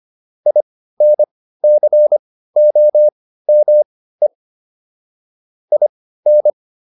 Morse code, heard as a keyed tone, about 25 wpm, spaced as for 15 wpm, 600 Hz. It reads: INCOME IN